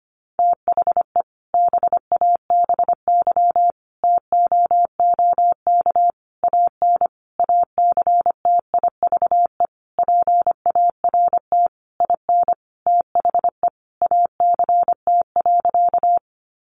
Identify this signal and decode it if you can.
Morse code, heard as a keyed tone, about 25 wpm, 700 Hz. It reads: THE BABY TOOK AN ACTIVE PART IN THE ACT.
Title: Sentences from Top 1300 Words 25wpm